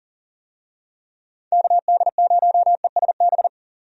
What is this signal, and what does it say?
Morse code, heard as a keyed tone, about 40 wpm, 700 Hz. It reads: KD0ESB
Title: USA Call Signs - 6 Characters - 40wpm